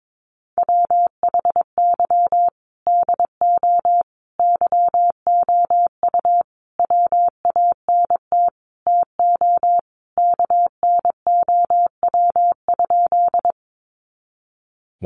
Morse code, heard as a keyed tone, about 22 wpm, 700 Hz. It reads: WHY DO YOU WANT TO KNOW?
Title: Sentences from Top 200 Words 22wpm